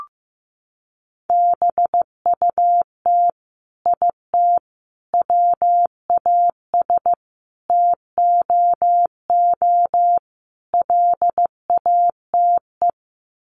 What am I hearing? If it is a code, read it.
Morse code, heard as a keyed tone, about 15 wpm, 700 Hz. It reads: BUT IT WAS TOO LATE
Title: Sets of 5 Words Encore2 15wpm